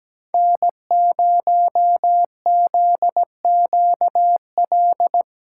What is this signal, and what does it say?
Morse code, heard as a keyed tone, about 17 wpm, 700 Hz. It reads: N0ZQL
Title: USA Callsign DE Callsign - 5 Characters - 17wpm